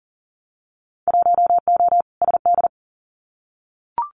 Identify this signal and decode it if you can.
Morse code, heard as a keyed tone, about 42 wpm, 700 Hz. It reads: 1O SD
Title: Winter Field Day Exchanges Encore 40wpm